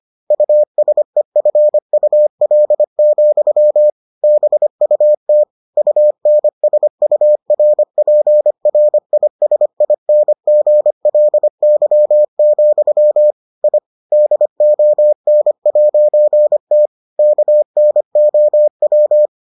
Morse code, heard as a keyed tone, about 25 wpm, 600 Hz. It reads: USEFUL, BUT UNSURPRISINGLY, I DON'T KNOW